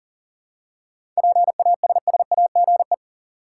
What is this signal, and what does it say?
Morse code, heard as a keyed tone, about 37 wpm, 700 Hz. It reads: PASSAGE
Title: Seven Letter Words - 40wpm